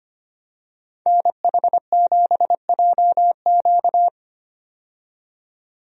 Morse code, heard as a keyed tone, about 25 wpm, 700 Hz. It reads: NH7JQ
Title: USA Call Signs - 5 Characters - 25wpm